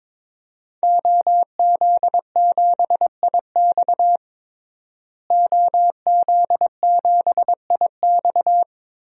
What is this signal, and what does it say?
Morse code, heard as a keyed tone, about 22 wpm, 700 Hz. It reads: OZ7IX OZ7IX
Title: DXCC-100 Call Signs - 5 Characters - Repeated 3x - 22wpm